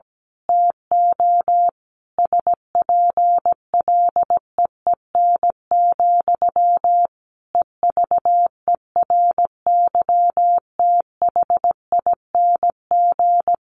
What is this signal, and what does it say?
Morse code, heard as a keyed tone, about 17 wpm, 700 Hz. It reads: TO SPLEEN, EVERYTHING
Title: A Sound Of Thunder 17wpm